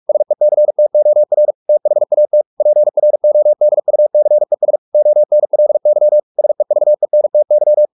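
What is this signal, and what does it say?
Morse code, heard as a keyed tone, about 44 wpm, 600 Hz. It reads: SECTOR THAT PRODUCES ONLY SEVENTY
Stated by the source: Morse practice at 45 wpm, the novel Never by Ken Follett in English, from CW med Gnister.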